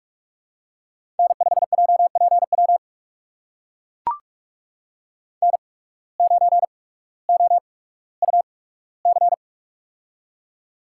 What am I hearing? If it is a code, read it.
Morse code, heard as a keyed tone, about 44 wpm, spaced as for 19 wpm, 700 Hz. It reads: N5JPW N9OUC
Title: USA Call Signs - 5 Characters - 8x Character Spacing - 45wpm